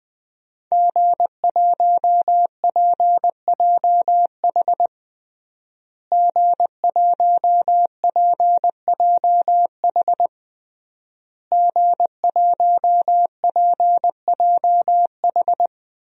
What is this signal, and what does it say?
Morse code, heard as a keyed tone, about 21 wpm, 700 Hz. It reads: G1PJH G1PJH G1PJH
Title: DXCC-100 Call Signs - 5 Characters - Repeated 3x - 20wpm